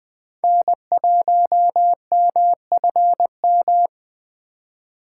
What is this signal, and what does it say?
Morse code, heard as a keyed tone, about 20 wpm, 700 Hz. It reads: N1MFM
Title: Speed Racing - USA 1x3 Call Signs 17wpm